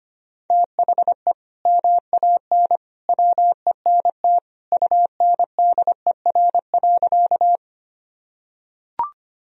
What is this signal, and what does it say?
Morse code, heard as a keyed tone, about 25 wpm, 700 Hz. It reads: THE MAN WENT UNDER.
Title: Sentences from Top 1000 Words - Review Part 1 - 25wpm